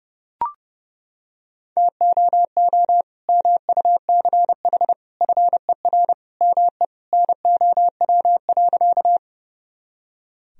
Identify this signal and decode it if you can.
Morse code, heard as a keyed tone, about 30 wpm, 700 Hz. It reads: TOO MUCH FER ME NOW.
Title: Sentences from Top 200 Words and Abbreviations 30wpm